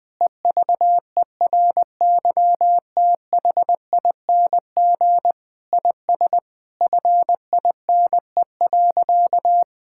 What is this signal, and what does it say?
Morse code, heard as a keyed tone, about 21 wpm, 700 Hz. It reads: EVERYTHING IS FINE.
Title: Sentences with 6-Word Phrases - 20wpm